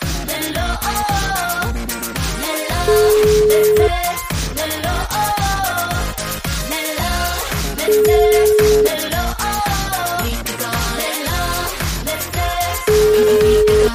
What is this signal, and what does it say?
Ringback tone (Europe, 425 Hz), with noise.